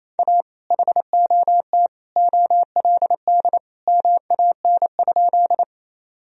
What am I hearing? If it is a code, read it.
Morse code, heard as a keyed tone, about 28 wpm, 700 Hz. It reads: A HOT OLD MAN?
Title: Sentences from Top 1000 Words - Review Part 3 - 28wpm